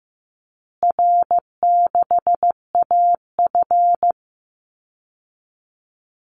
Morse code, heard as a keyed tone, about 15 wpm, 700 Hz. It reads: R6AF
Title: Easy Call Signs Repeated 3x - 15wpm